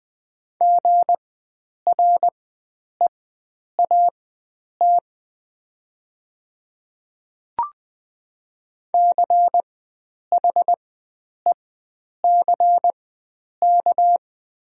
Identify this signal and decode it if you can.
Morse code, heard as a keyed tone, about 21 wpm, spaced as for 9 wpm, 700 Hz. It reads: GREAT CHECK
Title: 4x Character Spacing - Top 500 Words - 20wpm